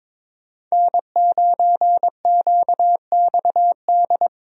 Morse code, heard as a keyed tone, about 22 wpm, 700 Hz. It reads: N9QXD
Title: USA Call Signs - 5 Characters - Repeated 2x - 22wpm